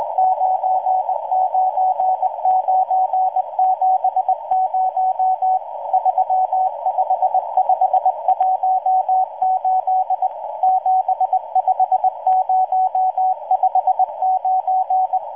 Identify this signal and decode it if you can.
Morse code, heard as a keyed tone, about 20 wpm, 750 Hz, with radio noise. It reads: 99970 35518 75059